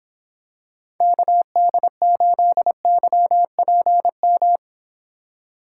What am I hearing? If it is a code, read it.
Morse code, heard as a keyed tone, about 26 wpm, 700 Hz. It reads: KD8YPM